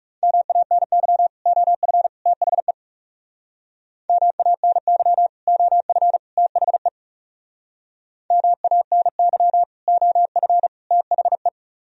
Morse code, heard as a keyed tone, about 39 wpm, 700 Hz. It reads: MANY OF THE MANY OF THE MANY OF THE